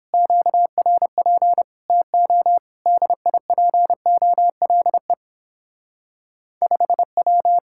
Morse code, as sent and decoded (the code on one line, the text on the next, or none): --.- .-. .--. / - --- / -.. .. .--. --- .-.. / ..... .--
QRP TO DIPOLE 5W